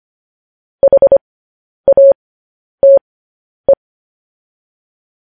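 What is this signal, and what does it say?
Morse code, heard as a keyed tone, about 25 wpm, spaced as for 10 wpm, 550 Hz. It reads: HATE
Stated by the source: Morse code news headlines practice transmission.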